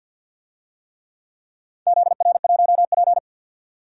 Morse code, heard as a keyed tone, about 52 wpm, 700 Hz. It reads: ZR1P
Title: DXCC-100 Call Signs - 4 Characters - 50wpm